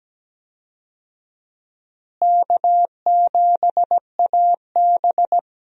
Morse code, heard as a keyed tone, about 17 wpm, 700 Hz. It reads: K7AB
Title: USA Call Signs - 4 Characters - Repeated 2x - 17wpm